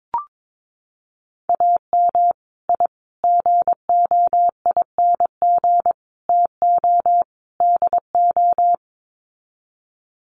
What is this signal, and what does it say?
Morse code, heard as a keyed tone, about 22 wpm, 700 Hz. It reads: AM I GOING TO DO
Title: Sets of 5 Words Encore 22wpm